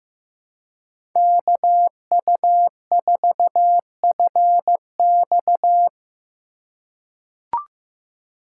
Morse code, heard as a keyed tone, about 15 wpm, 700 Hz. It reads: KU4FX
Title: USA Call Signs - 5 Characters - 3x Character Spacing - 15wpm